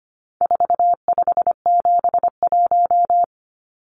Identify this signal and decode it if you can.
Morse code, heard as a keyed tone, about 25 wpm, 700 Hz. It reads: 4571
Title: Numbers - 4-Digit 25wpm